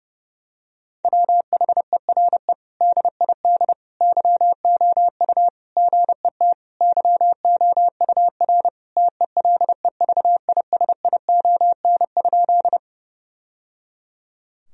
Morse code, heard as a keyed tone, about 30 wpm, 700 Hz. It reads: WHERE DID YOU GET YOUR TELEVISION?
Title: Sentences from Top 1000 Words - Review Part 4 - 30wpm